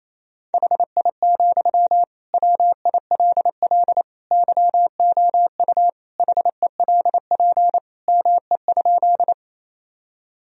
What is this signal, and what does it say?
Morse code, heard as a keyed tone, about 27 wpm, 700 Hz. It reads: HI, WILL YOU HELP ME?
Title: Sentences from Top 1000 Words - Review Part 3 - 28wpm